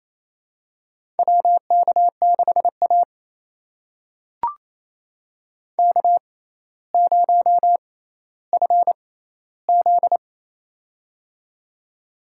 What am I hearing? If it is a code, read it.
Morse code, heard as a keyed tone, about 28 wpm, spaced as for 14 wpm, 700 Hz. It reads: WK6A K0FZ
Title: USA Call Signs - 4 Characters - 6x Character Spacing - 28wpm